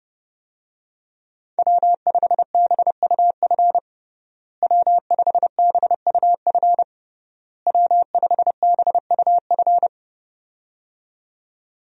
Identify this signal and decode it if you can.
Morse code, heard as a keyed tone, about 30 wpm, 700 Hz. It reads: W5BUF W5BUF W5BUF
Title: USA Call Signs - 5 Characters - Repeated 3x - 30wpm